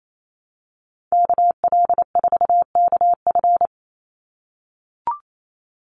Morse code, heard as a keyed tone, about 28 wpm, 700 Hz. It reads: KL4KF
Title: USA Call Signs 28wpm